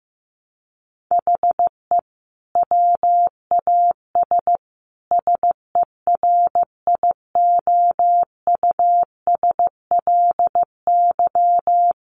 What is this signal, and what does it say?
Morse code, heard as a keyed tone, about 14 wpm, 700 Hz. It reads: HE WAS SERIOUSLY
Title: Sentences from Top 1200 Words 15wpm